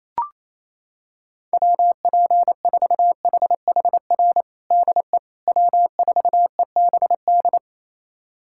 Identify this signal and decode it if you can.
Morse code, heard as a keyed tone, about 28 wpm, 700 Hz. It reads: WP4HHR DE W4EBD
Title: USA Callsign DE Callsign 28wpm